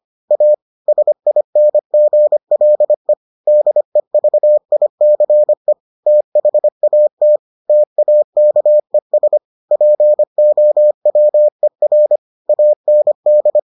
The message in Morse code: A SINGLE DEVICE THAT TAKES POWER AND

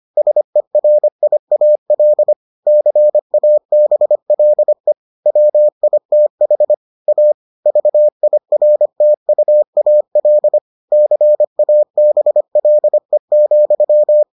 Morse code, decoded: SERIAL CABLE WITH A VIRTUAL CABLE,

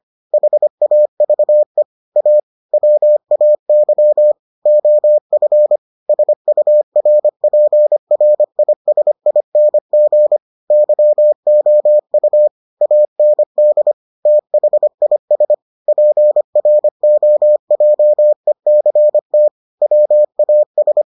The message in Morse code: HAVE A WAY OF SURPRISING YOU AND THIS PROJECT WAS

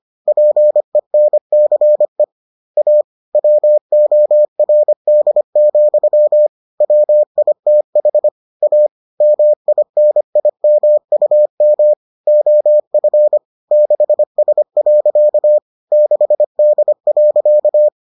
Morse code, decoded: PENCE A WORD, WITH A MINIMUM OF 6S. 6D.

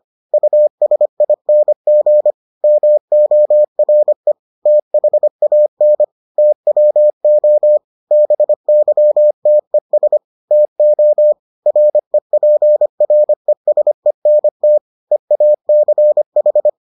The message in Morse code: USING MORE THAN TWO BYTES TO REPRESENT EACH

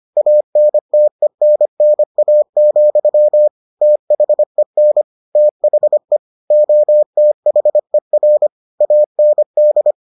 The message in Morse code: ANTENNA, THEN THE OTHER AND

